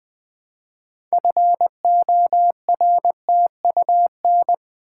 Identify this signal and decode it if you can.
Morse code, heard as a keyed tone, about 20 wpm, 700 Hz. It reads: FORTUN
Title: Build A Word - 3 Letters At A Time - 9-Letter Words - 20wpm